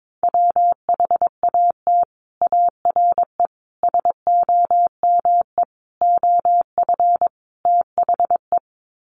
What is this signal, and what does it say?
Morse code, heard as a keyed tone, about 22 wpm, 700 Hz. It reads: WHAT ARE SOME OF THE
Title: Sets of 5 Words Encore 22wpm